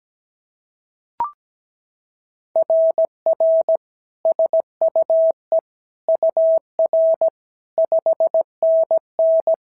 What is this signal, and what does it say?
Morse code, heard as a keyed tone, about 17 wpm, 650 Hz. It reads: RR SUE UR 5NN
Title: SOTA QSO Practice 17wpm